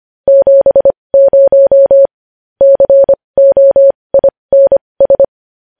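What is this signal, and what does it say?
Morse code, heard as a keyed tone, about 25 wpm, spaced as for 20 wpm, 550 Hz. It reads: 70 COINS
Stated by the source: Morse code news headlines practice at 20 wpm.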